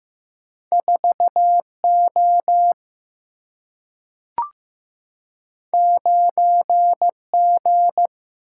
Morse code, heard as a keyed tone, about 15 wpm, 700 Hz. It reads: 4O 9G
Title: Single-Letter Single-Number 15wpm